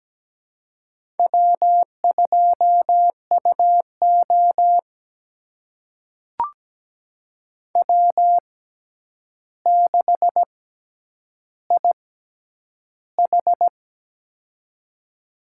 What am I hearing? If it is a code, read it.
Morse code, heard as a keyed tone, about 17 wpm, spaced as for 9 wpm, 700 Hz. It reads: W2UO W6IH
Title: USA Call Signs - 4 Characters - 6x Character Spacing - 17wpm